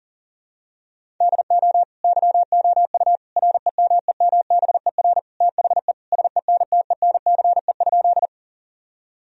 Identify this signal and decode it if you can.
Morse code, heard as a keyed tone, about 40 wpm, 700 Hz. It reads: DO YOU REMEMBER THE SENTENCE?
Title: Sentences from Top 1000 Words - Review Part 1 - 40wpm